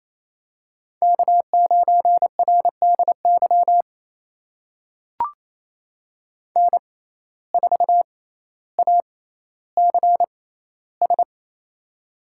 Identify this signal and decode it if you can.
Morse code, heard as a keyed tone, about 28 wpm, spaced as for 14 wpm, 700 Hz. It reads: K9RDY N4ACS
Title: USA Call Signs - 5 Characters - 6x Character Spacing - 28wpm